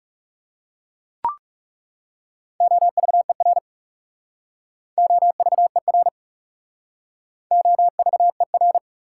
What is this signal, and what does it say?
Morse code, heard as a keyed tone, about 40 wpm, 690 Hz. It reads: OVER OVER OVER